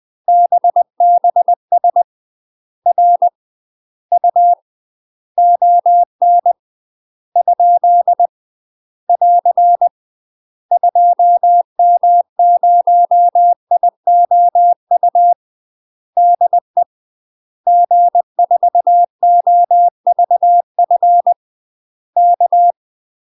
Morse code, heard as a keyed tone, about 20 wpm, 700 Hz. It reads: BBS R U ON ? <AR> 2M0IOU DE G4OVF K
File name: UK MASTER QSOs TRAINING/20WPM